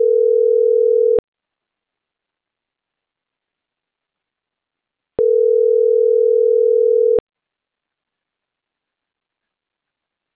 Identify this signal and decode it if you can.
Ringback tone (North America), 440 + 480 Hz.